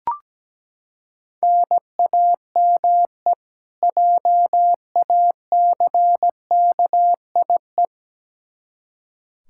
Morse code, heard as a keyed tone, about 17 wpm, 700 Hz. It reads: NAME JACKIE
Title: Practice QSO - Exchange 1 - Beginning - 17wpm